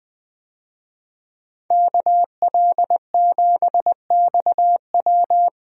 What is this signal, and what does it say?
Morse code, heard as a keyed tone, about 20 wpm, 700 Hz. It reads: KL7XW
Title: DXCC-100 Call Signs - 5 Characters - Repeated 2x - 20wpm